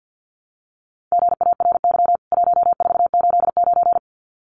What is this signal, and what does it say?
Morse code, heard as a keyed tone, about 48 wpm, 700 Hz. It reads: GARY 1489
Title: CWT Exchange 50wpm